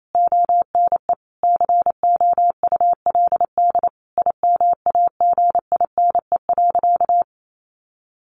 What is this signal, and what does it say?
Morse code, heard as a keyed tone, about 28 wpm, 700 Hz. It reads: ONE COULD IMAGINE.
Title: Sentences from Top 1400 Words 28wpm